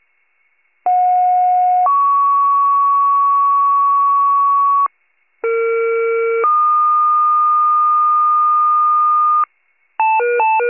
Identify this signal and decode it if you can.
Two-tone page, heard as a one-second tone, then a three-second tone, about 730 then 1120 Hz, about 460 then 1220 Hz. It is heard from 0.9 to 9.5 s.
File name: pager / 56 RESCUE / wcalert